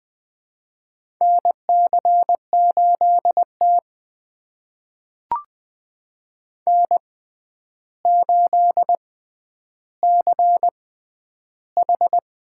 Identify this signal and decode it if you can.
Morse code, heard as a keyed tone, about 20 wpm, spaced as for 10 wpm, 700 Hz. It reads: NC8T N8CH
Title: USA Call Signs - 4 Characters - 6x Character Spacing - 20wpm